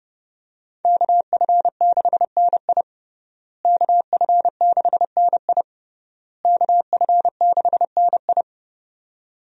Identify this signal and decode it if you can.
Morse code, heard as a keyed tone, about 30 wpm, 700 Hz. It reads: KF6NI KF6NI KF6NI
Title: USA Call Signs - 5 Characters - Repeated 3x - 30wpm